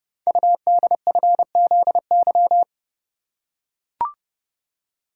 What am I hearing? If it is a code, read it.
Morse code, heard as a keyed tone, about 30 wpm, 700 Hz. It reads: UDFZY
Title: Code Groups - 5 Letters - 30wpm